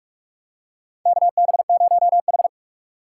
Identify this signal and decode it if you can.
Morse code, heard as a keyed tone, about 46 wpm, 700 Hz. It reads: KB0H
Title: USA Call Signs - 4 Characters - 4x Character Spacing - 45wpm